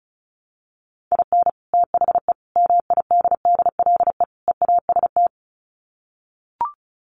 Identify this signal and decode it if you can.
Morse code, heard as a keyed tone, about 34 wpm, 700 Hz. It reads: IN THE MIDDLE EAST
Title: Sets of 4 Words 35wpm